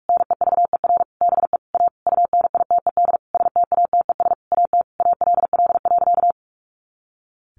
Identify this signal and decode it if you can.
Morse code, heard as a keyed tone, about 44 wpm, 700 Hz. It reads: NEVER BE A UNITED STATES AT ALL.
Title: A Sound Of Thunder 45wpm